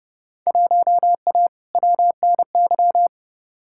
Morse code, heard as a keyed tone, about 31 wpm, 700 Hz. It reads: 1A WNY
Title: Field Day Exchanges - 30wpm